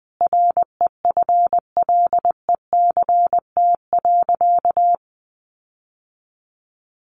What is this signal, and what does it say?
Morse code, heard as a keyed tone, about 20 wpm, 700 Hz. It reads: REFLECT.
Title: Sentences from Top 1500 Words 20wpm